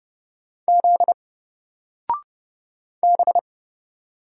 Morse code, heard as a keyed tone, about 30 wpm, 700 Hz. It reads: Z B
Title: Single Letter-Number 30wpm